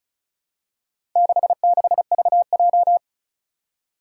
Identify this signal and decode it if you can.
Morse code, heard as a keyed tone, about 36 wpm, 700 Hz. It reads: 66VJ
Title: Four Challenging Letter-Number Combinations 35wpm